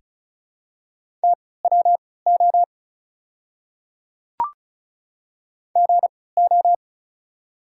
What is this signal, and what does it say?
Morse code, heard as a keyed tone, about 36 wpm, spaced as for 20 wpm, 700 Hz. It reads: TWO GO
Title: 3x Character Spacing - Top 100 Words - 35wpm